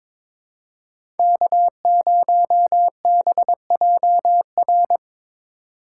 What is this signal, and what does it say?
Morse code, heard as a keyed tone, about 22 wpm, 700 Hz. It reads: K0BJR
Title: USA Call Signs - 4x Character Spacing - 22wpm